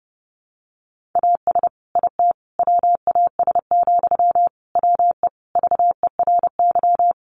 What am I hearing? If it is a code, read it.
Morse code, heard as a keyed tone, about 30 wpm, 700 Hz. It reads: AS IT WAS, WE VERY
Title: Sentences from Top 1400 Words 30wpm